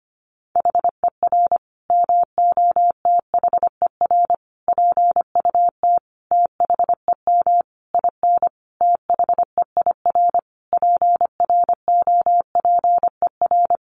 Morse code, heard as a keyed tone, about 25 wpm, 700 Hz. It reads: HER MOTHER PUT THEM IN THEIR PROPER
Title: Sentences from Top 800 Words 25wpm